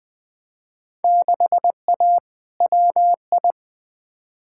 Morse code, heard as a keyed tone, about 20 wpm, 700 Hz. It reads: 6A WI
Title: Field Day Exchanges - 20wpm